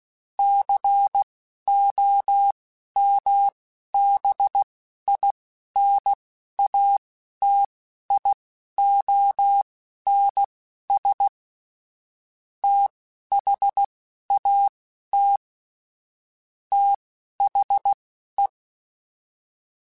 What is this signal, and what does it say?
Morse code, heard as a keyed tone, about 16 wpm, spaced as for 11 wpm, 800 Hz. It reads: COMBINATIONS THAT THE
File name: Technical Subjects / Technical Subjects-mp3s / AND Gate @10